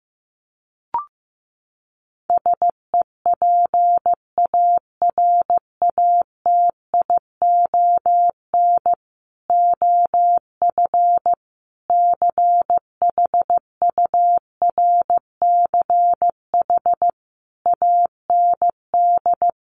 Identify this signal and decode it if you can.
Morse code, heard as a keyed tone, about 15 wpm, 700 Hz. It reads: SEPARATION OF CHURCH AND
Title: Sets of 5 Words 15wpm